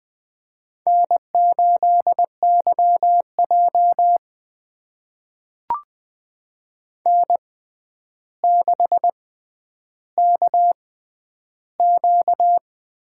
Morse code, heard as a keyed tone, about 20 wpm, spaced as for 10 wpm, 700 Hz. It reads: N8YJ N6KQ
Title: USA Call Signs - 4 Characters - 6x Character Spacing - 20wpm